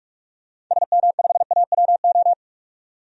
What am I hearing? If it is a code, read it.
Morse code, heard as a keyed tone, about 42 wpm, 700 Hz. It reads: SM5AWO
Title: DXCC-100 Call Signs - 6 Characters - 45wpm